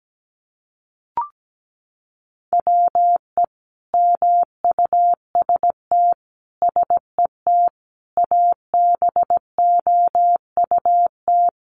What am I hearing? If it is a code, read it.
Morse code, heard as a keyed tone, about 17 wpm, 700 Hz. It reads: WE MUST SET ABOUT
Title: Sentences from Top 1400 Words 17wpm